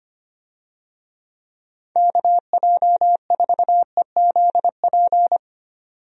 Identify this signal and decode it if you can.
Morse code, heard as a keyed tone, about 25 wpm, 700 Hz. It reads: KJ4EZP
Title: USA Call Signs - 6 Characters - 25wpm